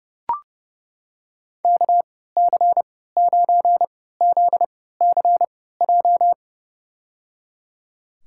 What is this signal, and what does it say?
Morse code, heard as a keyed tone, about 31 wpm, spaced as for 17 wpm, 700 Hz. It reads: KC9ZCJ
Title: USA Call Signs - 3x Character Spacing - 30wpm